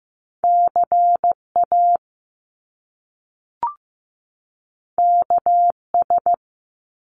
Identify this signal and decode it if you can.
Morse code, heard as a keyed tone, about 15 wpm, 700 Hz. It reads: CA KS
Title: US States 15wpm